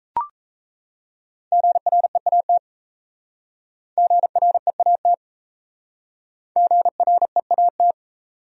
Morse code, heard as a keyed tone, about 34 wpm, 700 Hz. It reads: GREAT GREAT GREAT
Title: Speed-Racing Top 200 Words 28wpm